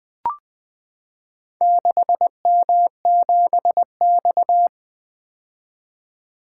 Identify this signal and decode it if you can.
Morse code, heard as a keyed tone, about 20 wpm, 700 Hz. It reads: 6M7X
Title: Four Letter-Number Combinations - 20wpm